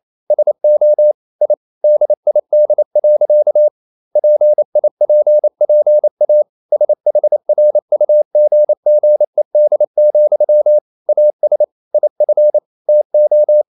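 Morse code, heard as a keyed tone, about 28 wpm, 600 Hz. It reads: SO I DID. PIPPA SHRUGGED, AS IF TO
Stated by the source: Morse code practice at 28 wpm, text of Ken Follett's novel Never in English, from CW med Gnister.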